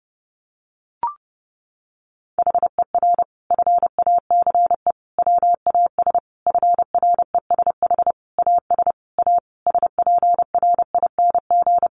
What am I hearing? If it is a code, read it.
Morse code, heard as a keyed tone, about 30 wpm, 700 Hz. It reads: HER FACE WAS FRESH AS A SPRING
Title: Sentences from Top 1200 Words 30wpm